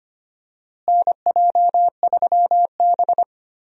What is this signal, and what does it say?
Morse code, heard as a keyed tone, about 25 wpm, 700 Hz. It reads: NJ3B
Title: USA Call Signs - 4 Characters - Repeated 2x - 25wpm